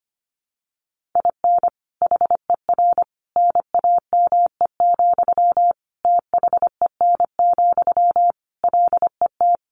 Morse code, heard as a keyed tone, about 25 wpm, 700 Hz. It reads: IN HER NAME, THEN, LET
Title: Sentences from Top 500 Words 25wpm